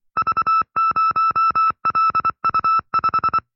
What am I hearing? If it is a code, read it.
Morse code, heard as a keyed tone, about 24 wpm, 1310 Hz. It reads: V0LU5
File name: FreeFiles / CW-practice / 20WPM-side2